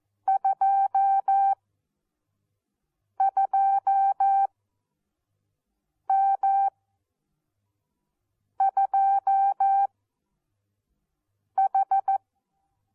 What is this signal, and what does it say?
Morse code, heard as a keyed tone, about 14 wpm, spaced as for 4 wpm, 780 Hz. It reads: 22M2H
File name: Onlinelehrgang / morsen / DJ4UF Morsen14